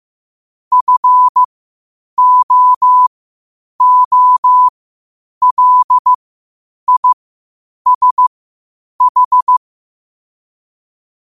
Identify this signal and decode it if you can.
Morse code, heard as a keyed tone, about 15 wpm, spaced as for 8 wpm, 1000 Hz. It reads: FOOLISH
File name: qotdfiles / 8wpm